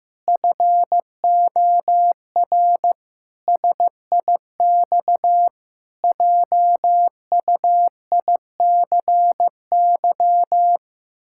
Morse code, heard as a keyed tone, about 15 wpm, 700 Hz. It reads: FOR SIX JUICY
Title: Pangram Challenge v2 - 15wpm